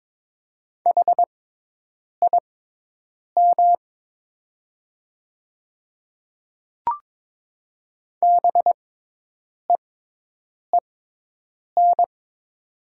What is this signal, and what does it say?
Morse code, heard as a keyed tone, about 21 wpm, spaced as for 8 wpm, 690 Hz. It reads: HIM BEEN
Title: 6x Character Spacing - Top 200 Words - 22wpm